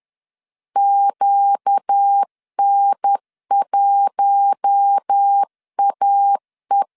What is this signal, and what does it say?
Morse code, heard as a keyed tone, about 10 wpm, 800 Hz. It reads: QN1AE
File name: CW / DET / 1 6 1500C S B 10wpm 4137